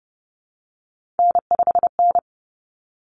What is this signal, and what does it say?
Morse code, heard as a keyed tone, about 30 wpm, 700 Hz. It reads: N5N
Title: Common Cut Numbers - 3-Digit 30wpm